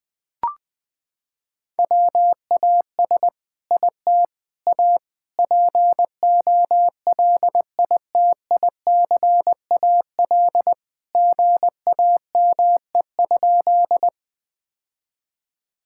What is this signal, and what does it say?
Morse code, heard as a keyed tone, about 20 wpm, 700 Hz. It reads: WAS IT A POLITICAL GAME?
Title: Sentences from Top 1000 Words - Review Part 5 - 20wpm